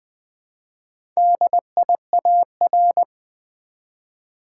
Morse code, heard as a keyed tone, about 20 wpm, 700 Hz. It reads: DIAR
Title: Build A Word - 1 Letter At A Time - 3 to 5-Letter Words - 20wpm